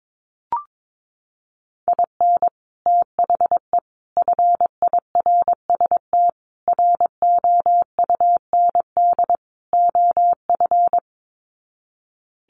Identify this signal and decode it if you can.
Morse code, heard as a keyed tone, about 22 wpm, 700 Hz. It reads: IN THE FIRST ROUND OF
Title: Sets of 5 Words Encore 22wpm